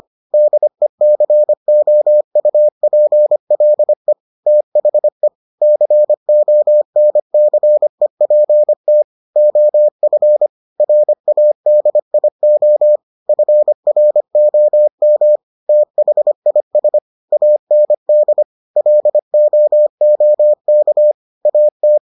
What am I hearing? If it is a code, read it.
Morse code, heard as a keyed tone, about 25 wpm, 600 Hz. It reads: DECOUPLE THE CONCEPT OF RADIO FROM THIS AND LOOK AT